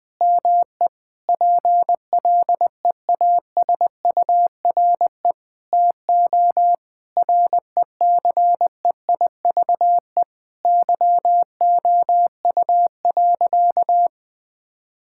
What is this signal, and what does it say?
Morse code, heard as a keyed tone, about 20 wpm, 700 Hz. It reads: ME PLEASURE TO RECEIVE YOU.
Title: Sentences from Top 1600 Words 20wpm